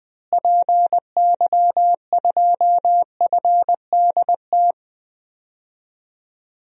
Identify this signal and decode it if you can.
Morse code, heard as a keyed tone, about 21 wpm, 700 Hz. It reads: PY2FDT
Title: DXCC-100 Call Signs - 6 Characters - Repeated 2x - 20wpm